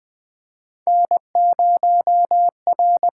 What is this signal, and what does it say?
Morse code, heard as a keyed tone, about 20 wpm, 700 Hz. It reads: N0R
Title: USA Call Signs - 6x Character Spacing - 20wpm